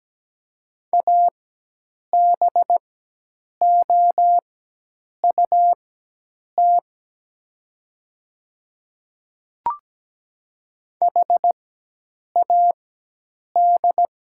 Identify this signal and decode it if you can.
Morse code, heard as a keyed tone, about 17 wpm, spaced as for 8 wpm, 700 Hz. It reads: ABOUT HAD
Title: 4x Character Spacing - Top 300 Words - 17wpm